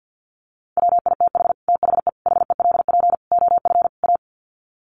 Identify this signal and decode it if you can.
Morse code, heard as a keyed tone, about 48 wpm, 700 Hz. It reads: WITH THE HELP OF A
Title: Sets of 5 Words 50wpm